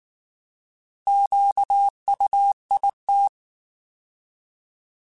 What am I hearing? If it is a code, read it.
Morse code, heard as a keyed tone, about 19 wpm, 800 Hz. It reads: QUIT
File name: Alphabetical / Alphabetical mp3s / Qs @18